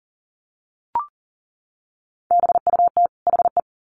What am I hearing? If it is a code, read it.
Morse code, heard as a keyed tone, about 40 wpm, 700 Hz. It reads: BUT HE
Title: Sets of 2 Words 40wpm